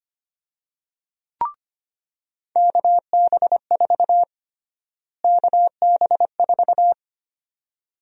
Morse code, heard as a keyed tone, about 25 wpm, 700 Hz. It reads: KB4 KB4
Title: USA Call Sign Prefixes - Repeated 2x - 25wpm